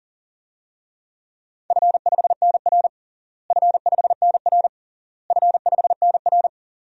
Morse code, heard as a keyed tone, about 40 wpm, 700 Hz. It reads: F5NR F5NR F5NR
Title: DXCC-100 Call Signs - 4 Characters - Repeated 3x - 40wpm